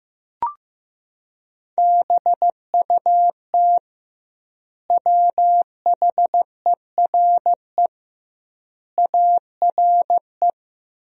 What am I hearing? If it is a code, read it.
Morse code, heard as a keyed tone, about 15 wpm, 700 Hz. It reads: BUT WHERE ARE
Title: 2x Word-Spacing - Sentences from Top 200 Words - 15wpm